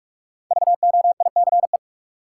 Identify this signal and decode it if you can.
Morse code, heard as a keyed tone, about 46 wpm, 700 Hz. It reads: VOICE
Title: Sentences from Top 500 Words - New Word Drill - 45wpm